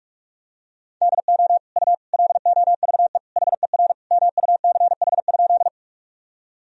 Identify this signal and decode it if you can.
Morse code, heard as a keyed tone, about 46 wpm, 700 Hz. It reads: DO U LOVE HER MUCH?
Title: Sentences from Top 400 Words and Abbreviations 45wpm